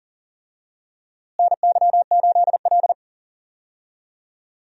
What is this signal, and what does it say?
Morse code, heard as a keyed tone, about 40 wpm, 700 Hz. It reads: NY8L
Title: USA Call Signs - 4 Characters - Repeated 2x - 40wpm